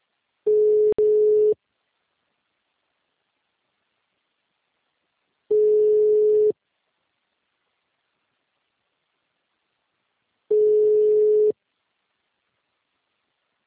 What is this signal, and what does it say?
Ringback tone (Europe, 425 Hz).
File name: out- -2001- - - .28105